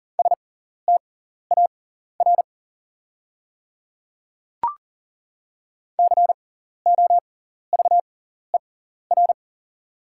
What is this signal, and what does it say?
Morse code, heard as a keyed tone, about 40 wpm, spaced as for 14 wpm, 690 Hz. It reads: STAR COVER
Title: 6x Character Spacing - Top 500 Words - 40wpm